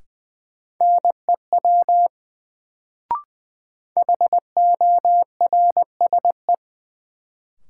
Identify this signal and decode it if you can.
Morse code, heard as a keyed tone, about 20 wpm, 700 Hz. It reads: NEW HORSE
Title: Sentences from Top 300 Words - New Word Drill - 20wpm